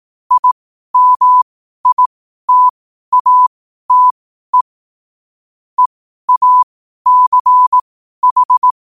Morse code, heard as a keyed tone, about 18 wpm, spaced as for 12 wpm, 1000 Hz. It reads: IMITATE EACH
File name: qotdfiles / 12wpm